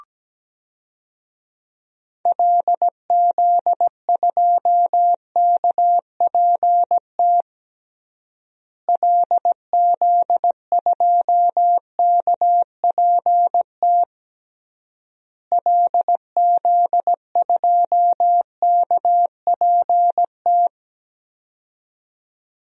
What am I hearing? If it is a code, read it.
Morse code, heard as a keyed tone, about 17 wpm, 700 Hz. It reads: LZ2KPT LZ2KPT LZ2KPT
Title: DXCC-100 Call Signs - 6 Characters - Repeated 3x - 17wpm